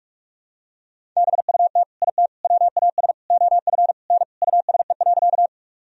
Morse code, heard as a keyed tone, about 42 wpm, 700 Hz. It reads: BUT IT WAS OF N USE.
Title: Sentences from Top 100 Words and Abbreviations 45wpm